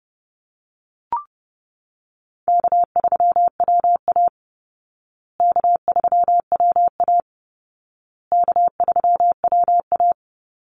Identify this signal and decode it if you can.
Morse code, heard as a keyed tone, about 30 wpm, 700 Hz. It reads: K3WA K3WA K3WA
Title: Easy Call Signs Repeated 3x - 30wpm